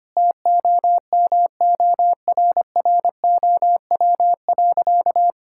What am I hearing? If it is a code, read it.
Morse code, heard as a keyed tone, about 25 wpm, 700 Hz. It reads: TOMORROW.
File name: practice / 25wpm / Sentences with 2-Word Phrases - 25wpm